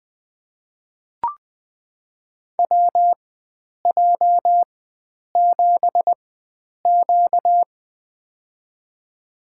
Morse code, heard as a keyed tone, about 20 wpm, spaced as for 9 wpm, 700 Hz. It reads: WJ7Q
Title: USA Call Signs - 4 Characters - 4x Character Spacing - 20wpm